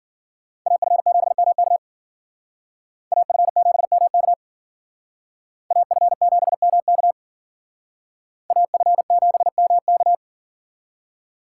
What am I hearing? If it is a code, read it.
Morse code, heard as a keyed tone, about 51 wpm, 700 Hz. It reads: AF7MK AF7MK AF7MK AF7MK